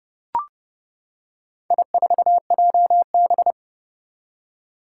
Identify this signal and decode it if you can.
Morse code, heard as a keyed tone, about 30 wpm, 700 Hz. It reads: I4JB
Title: Four Challenging Letter-Number Combinations 30wpm